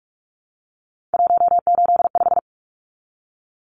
Morse code, heard as a keyed tone, about 44 wpm, 700 Hz. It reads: 185
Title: Numbers - 3-Digit 45wpm